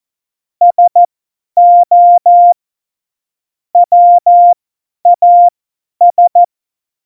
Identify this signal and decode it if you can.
Morse code, heard as a keyed tone, about 14 wpm, spaced as for 10 wpm, 700 Hz. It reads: SO WAS
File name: Morsekurs / Witze / Witz 28. November 2017, 50 F70